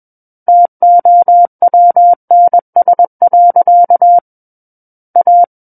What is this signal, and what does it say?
Morse code, heard as a keyed tone, about 21 wpm, 700 Hz. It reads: TOWNS. A
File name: Book One / 12-Chapter 12 - 21 WPM